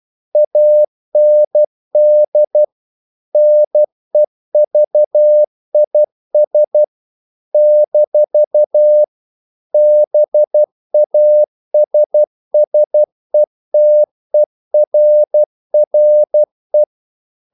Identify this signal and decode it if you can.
Morse code, heard as a keyed tone, about 12 wpm, 600 Hz. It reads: AND NEVIS - BASSETERRE
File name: CW / Practice / NationalCapitals12